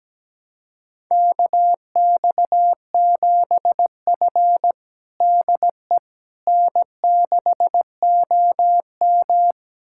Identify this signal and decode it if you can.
Morse code, heard as a keyed tone, about 17 wpm, 700 Hz. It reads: KX7F DE N6OM
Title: USA Callsign DE Callsign - 4 Characters - 17wpm